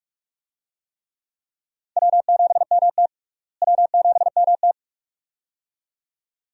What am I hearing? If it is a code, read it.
Morse code, heard as a keyed tone, about 43 wpm, 700 Hz. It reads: W7MT W7MT
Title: USA Call Signs - 4 Characters - Repeated 2x - 45wpm